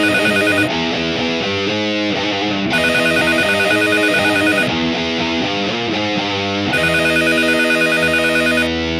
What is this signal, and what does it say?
There is no code.